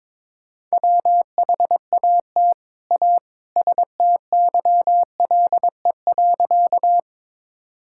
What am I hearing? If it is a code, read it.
Morse code, heard as a keyed tone, about 22 wpm, 700 Hz. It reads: WHAT A STYLE.
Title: Sentences from Top 1000 Words - Review Part 5 - 22wpm